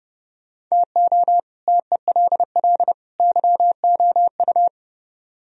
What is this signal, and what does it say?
Morse code, heard as a keyed tone, about 39 wpm, 700 Hz. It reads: TO TELL YOU